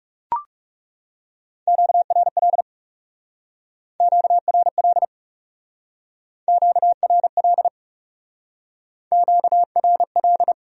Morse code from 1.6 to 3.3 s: QRL

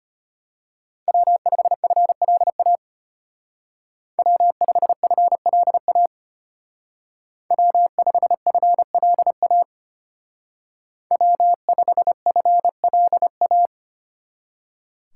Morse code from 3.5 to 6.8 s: W5FLA

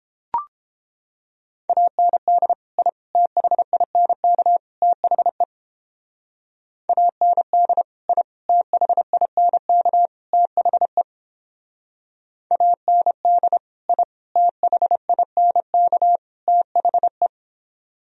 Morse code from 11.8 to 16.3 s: AND I THINK

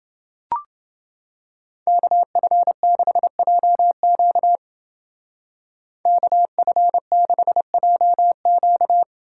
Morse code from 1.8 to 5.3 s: KF6JQ

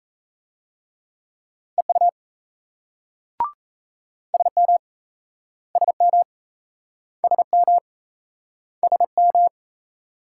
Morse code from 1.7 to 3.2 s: EU

